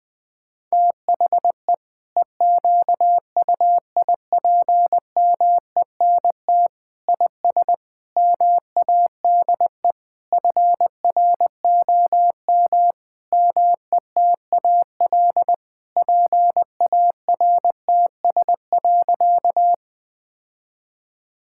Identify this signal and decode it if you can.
Morse code, heard as a keyed tone, about 20 wpm, 700 Hz. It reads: THE EQUIPMENT IS MADE FROM METAL PARTS.
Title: Sentences from Top 1000 Words - Review Part 6 - 20wpm